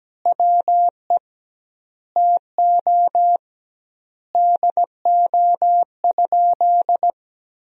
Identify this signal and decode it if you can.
Morse code, heard as a keyed tone, about 17 wpm, 700 Hz. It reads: WE TO DO?
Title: 2x Word-Spacing - Sentences from Top 100 Words - 17wpm